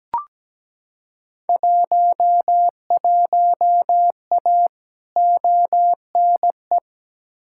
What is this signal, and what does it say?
Morse code, heard as a keyed tone, about 17 wpm, 700 Hz. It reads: 11A ONE
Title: Field Day Exchanges Encore - 17wpm